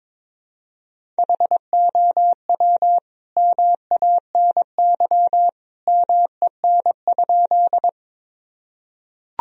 Morse code, heard as a keyed tone, about 21 wpm, 700 Hz. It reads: HOW MANY MEN?
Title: Sentences from Top 1000 Words - Review Part 1 - 22wpm